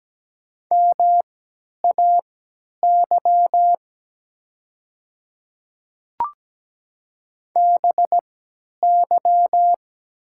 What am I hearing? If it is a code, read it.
Morse code, heard as a keyed tone, about 17 wpm, spaced as for 10 wpm, 700 Hz. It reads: MAY BY
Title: 3x Character Spacing - Top 100 Words - 17wpm